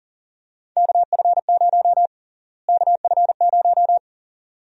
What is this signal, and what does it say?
Morse code, heard as a keyed tone, about 40 wpm, 700 Hz. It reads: KF0 KF0
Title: USA Call Sign Prefixes - Repeated 2x - 40wpm